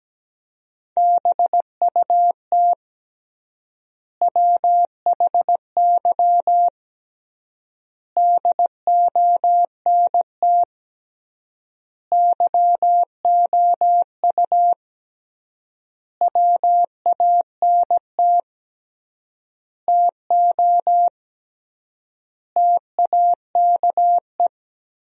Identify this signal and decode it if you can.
Morse code, heard as a keyed tone, about 17 wpm, 700 Hz. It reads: BUT WHY DONT YOU WANT TO TAKE
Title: 3x Word-Spacing - Sentences from Top 300 Words - 17wpm